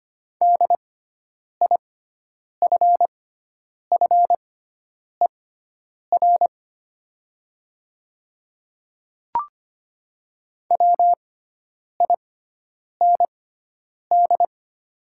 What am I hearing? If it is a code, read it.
Morse code, heard as a keyed tone, about 25 wpm, spaced as for 9 wpm, 700 Hz. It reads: DIFFER WIND
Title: 6x Character Spacing - Top 500 Words - 25wpm